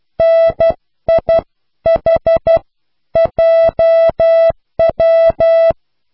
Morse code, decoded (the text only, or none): NIHJW